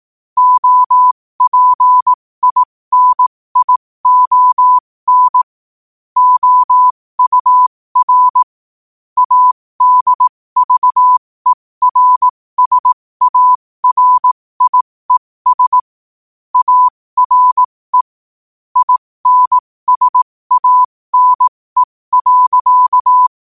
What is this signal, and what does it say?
Morse code, heard as a keyed tone, about 18 wpm, spaced as for 15 wpm, 1000 Hz. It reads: OPINION OUR ADVERSARIES ARE INSANE.
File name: qotdfiles / 15wpm